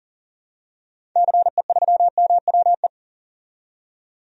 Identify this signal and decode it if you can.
Morse code, heard as a keyed tone, about 40 wpm, 700 Hz. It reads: CE3MWE
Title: DXCC-100 Call Signs - 6 Characters - 40wpm